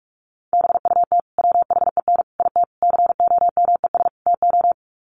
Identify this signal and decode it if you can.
Morse code, heard as a keyed tone, about 44 wpm, 700 Hz. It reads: BUT WHEN IT COMES TO
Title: Sets of 5 Words 45wpm